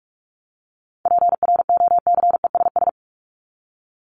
Morse code, heard as a keyed tone, about 44 wpm, 700 Hz. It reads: PROCESS
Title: ING Suffix - Top 100 Words with Base Words 45wpm